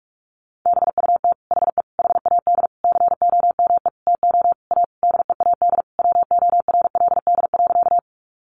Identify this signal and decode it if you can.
Morse code, heard as a keyed tone, about 44 wpm, 700 Hz. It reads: BUT HE HAD COME TO A DEAD WORLD.
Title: Sentences from Top 1500 Words 45wpm